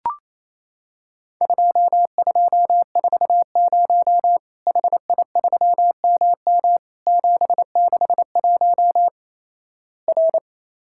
Morse code, heard as a keyed tone, about 28 wpm, 700 Hz. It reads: 2240 HI3MM 761 R
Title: Worked All Europe DX Contest - QTC Exchange - 28wpm